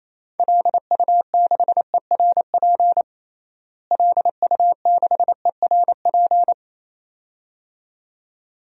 Morse code, heard as a keyed tone, about 28 wpm, 700 Hz. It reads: LU6ERP LU6ERP
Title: DXCC-100 Call Signs - 6 Characters - Repeated 2x - 28wpm